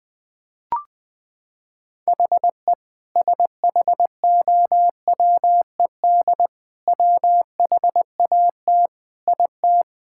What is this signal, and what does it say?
Morse code, heard as a keyed tone, about 20 wpm, 700 Hz. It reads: HE SHOWED WHAT IT
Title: Sentences with 5-Word Phrases - 20wpm